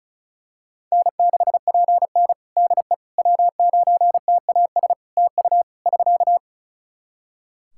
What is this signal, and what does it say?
Morse code, heard as a keyed tone, about 34 wpm, 700 Hz. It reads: N6PN DE W9TAS TU <SK>
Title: Practice QSO - Exchange 4 - Ending - 35wpm